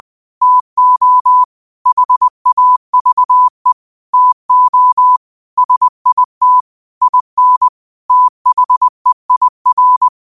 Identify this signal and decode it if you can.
Morse code, heard as a keyed tone, about 20 wpm, 1000 Hz. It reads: TO HAVE TO SIT IN THEIR